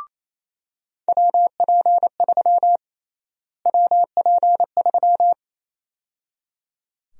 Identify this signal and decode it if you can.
Morse code, heard as a keyed tone, about 27 wpm, 700 Hz. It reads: WP3 WP3
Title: USA Call Sign Prefixes - Repeated 2x - 28wpm